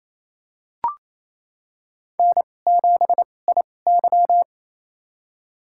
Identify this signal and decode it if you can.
Morse code, heard as a keyed tone, about 28 wpm, spaced as for 20 wpm, 700 Hz. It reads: N7IY
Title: USA Call Signs - 2x Character Spacing - 28wpm